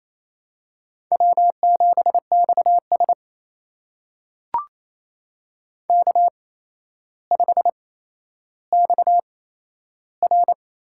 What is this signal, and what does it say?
Morse code, heard as a keyed tone, about 27 wpm, spaced as for 12 wpm, 700 Hz. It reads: W7XS K5XR
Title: USA Call Signs - 4 Characters - 8x Character Spacing - 28wpm